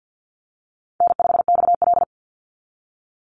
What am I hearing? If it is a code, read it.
Morse code, heard as a keyed tone, about 48 wpm, 700 Hz. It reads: N5XL
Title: NA SOTA Chaser Callsigns 50wpm